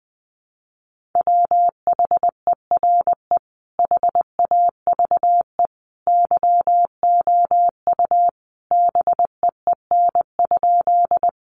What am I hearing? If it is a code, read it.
Morse code, heard as a keyed tone, about 20 wpm, 700 Hz. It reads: WHERE HAVE YOU BEEN?
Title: Sentences from Top 200 Words 20wpm